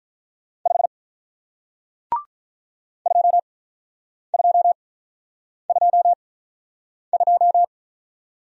Morse code, heard as a keyed tone, about 48 wpm, spaced as for 9 wpm, 700 Hz. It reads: 5 2222